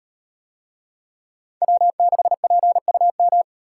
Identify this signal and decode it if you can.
Morse code, heard as a keyed tone, about 36 wpm, 700 Hz. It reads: W6PUM